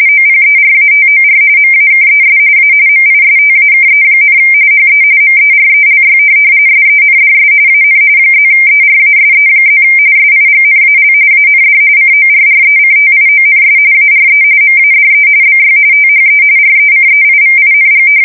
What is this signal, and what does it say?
RTTY, about 45.45 baud. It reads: OPERATOR MAKES A MISTAKE, HE MERELY SENDS AN +ERASE+ SIGNAL AND CORRECTIONS ARE MADE BY THE RECEIVING
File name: fsk170 / printing-telegraph